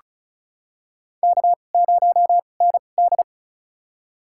Morse code, heard as a keyed tone, about 35 wpm, spaced as for 25 wpm, 700 Hz. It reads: K0ND
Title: USA Call Signs - 4 Characters - 2x Character Spacing - 35wpm